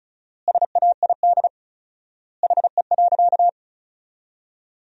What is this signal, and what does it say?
Morse code, heard as a keyed tone, about 34 wpm, 700 Hz. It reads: SAID HE.
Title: 4x Word-Spacing - Sentences from Top 400 Words - 35wpm